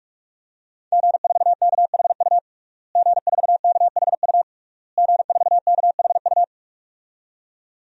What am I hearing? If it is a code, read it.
Morse code, heard as a keyed tone, about 46 wpm, 700 Hz. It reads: G4KHU G4KHU G4KHU
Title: DXCC-100 Call Signs - 5 Characters - Repeated 3x - 45wpm